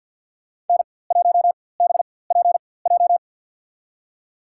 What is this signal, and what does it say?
Morse code, heard as a keyed tone, about 53 wpm, spaced as for 24 wpm, 700 Hz. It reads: N1BPJ
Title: USA Call Signs - 4x Character Spacing - 50wpm